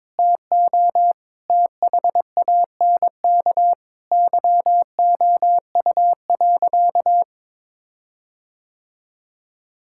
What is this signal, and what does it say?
Morse code, heard as a keyed tone, about 22 wpm, 700 Hz. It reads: TO THANK YOU.
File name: practice / 22wpm / Sentences with 8-Word Phrases - 22wpm